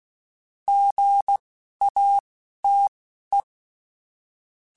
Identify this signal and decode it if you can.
Morse code, heard as a keyed tone, about 16 wpm, spaced as for 12 wpm, 800 Hz. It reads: GATE
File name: Technical Subjects / Technical Subjects-mp3s / OR Gate @10